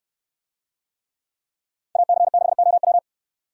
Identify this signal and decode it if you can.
Morse code, heard as a keyed tone, about 69 wpm, 700 Hz. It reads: AC6OW